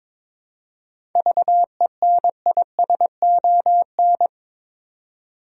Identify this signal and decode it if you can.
Morse code, heard as a keyed tone, about 22 wpm, 700 Hz. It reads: VENISON
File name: Difficult Words with Hints - 22wpm